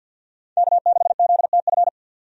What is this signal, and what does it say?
Morse code, heard as a keyed tone, about 50 wpm, 700 Hz. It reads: K6ZTF
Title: USA Call Signs - 5 Characters - 50wpm